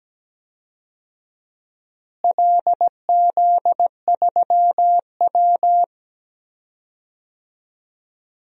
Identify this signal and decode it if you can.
Morse code, heard as a keyed tone, about 17 wpm, 700 Hz. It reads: LZ3W
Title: DXCC-100 Call Signs - 4 Characters - Repeated 2x - 17wpm